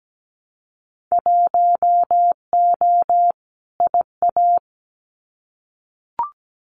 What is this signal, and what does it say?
Morse code, heard as a keyed tone, about 17 wpm, 700 Hz. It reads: 1O IA
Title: Winter Field Day Exchanges Encore 17wpm